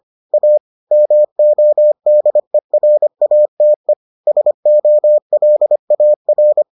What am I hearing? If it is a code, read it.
Morse code, heard as a keyed tone, about 25 wpm, 600 Hz. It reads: A MODERATE SOLAR